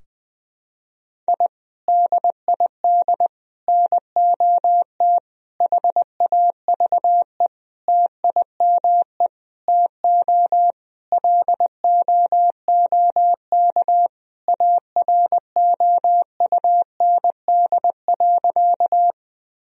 Morse code, heard as a keyed tone, about 20 wpm, 700 Hz. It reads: I DID NOT HAVE TIME TO LOOK AROUND.
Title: Sentences from Top 1000 Words - Review Part 3 - 20wpm